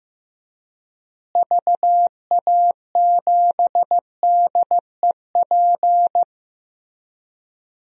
Morse code, heard as a keyed tone, about 15 wpm, 700 Hz. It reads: VA7DEP
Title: DXCC-100 Call Signs - 6 Characters - Repeated 3x - 15wpm